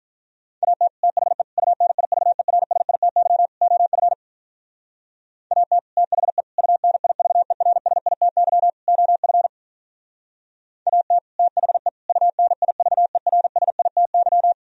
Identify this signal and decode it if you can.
Morse code, heard as a keyed tone, about 46 wpm, 700 Hz. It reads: AT THE UNIVERSITY OF AT THE UNIVERSITY OF AT THE UNIVERSITY